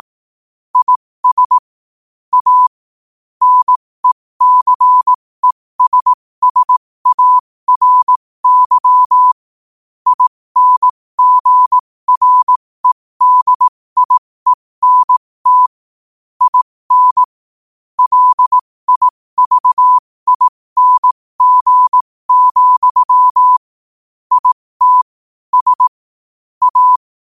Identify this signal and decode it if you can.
Morse code, heard as a keyed tone, about 18 wpm, spaced as for 15 wpm, 1000 Hz. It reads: IS A NECESSARY INGREDIENT IN LIVING, IT S A
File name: qotdfiles / 15wpm